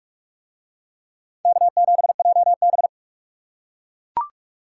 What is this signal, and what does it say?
Morse code, heard as a keyed tone, about 40 wpm, 700 Hz. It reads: K7JB